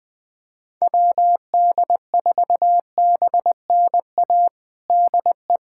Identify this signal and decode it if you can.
Morse code, heard as a keyed tone, about 20 wpm, 700 Hz. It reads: WD4BNA DE
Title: USA Callsign DE Callsign - 6 Characters - 20wpm